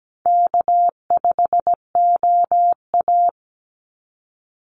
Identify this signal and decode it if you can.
Morse code, heard as a keyed tone, about 17 wpm, 700 Hz. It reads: K5OA
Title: Easy Call Signs Repeated 2x - 17wpm